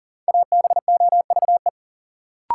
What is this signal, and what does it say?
Morse code, heard as a keyed tone, about 37 wpm, 700 Hz. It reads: ABOVE
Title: Sentences from Top 1000 Words - Review Part 2 - New Word Drill - 40wpm